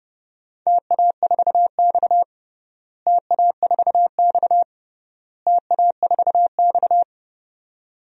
Morse code, heard as a keyed tone, about 29 wpm, 700 Hz. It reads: TA4X TA4X TA4X
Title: DXCC-100 Call Signs - 4 Characters - Repeated 3x - 30wpm